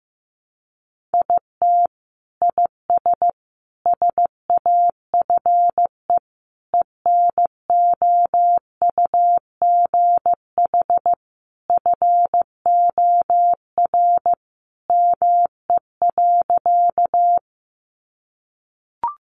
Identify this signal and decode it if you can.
Morse code, heard as a keyed tone, about 14 wpm, 700 Hz. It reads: IT IS SAFE ENOUGH FOR ME.
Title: Sentences from Top 900 Words 15wpm